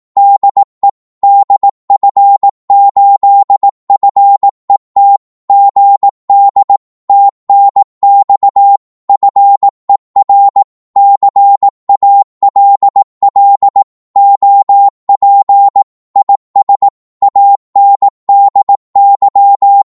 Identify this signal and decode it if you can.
Morse code, heard as a keyed tone, about 18 wpm, 800 Hz. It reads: DE DF8FET GD TNX FER CALL OP IS ANDY
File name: Morse / qso-18wpm